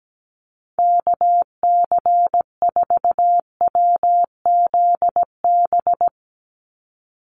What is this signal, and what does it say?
Morse code, heard as a keyed tone, about 17 wpm, 700 Hz. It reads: KC4WZB
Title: NA SOTA Activator Callsigns 17wpm